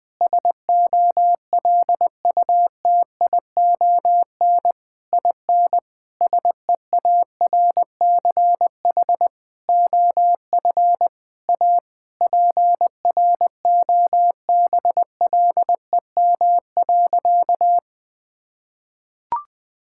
Morse code, heard as a keyed tone, about 20 wpm, 700 Hz. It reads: SOLUTION IN SEARCH OF A PROBLEM.
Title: Sentences with 8-Word Phrases - Spoken Beginning - 20wpm